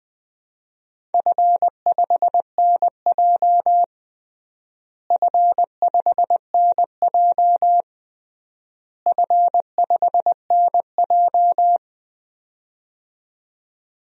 Morse code, heard as a keyed tone, about 21 wpm, 700 Hz. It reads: F5NJ F5NJ F5NJ
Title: DXCC-100 Call Signs - 4 Characters - Repeated 3x - 20wpm